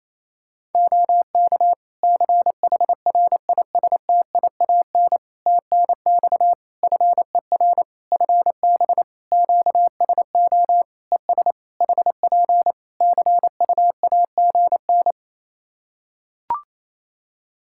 Morse code, heard as a keyed tone, about 28 wpm, 700 Hz. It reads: OK CHRISTIAN TNX FER FB QSO ES HP CUAGN
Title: Practice QSO - Exchange 4 - Ending - 28wpm